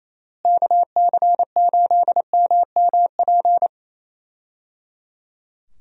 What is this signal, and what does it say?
Morse code, heard as a keyed tone, about 27 wpm, 700 Hz. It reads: KC8MMP
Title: USA Call Signs - 6 Characters - Repeated 3x - 28wpm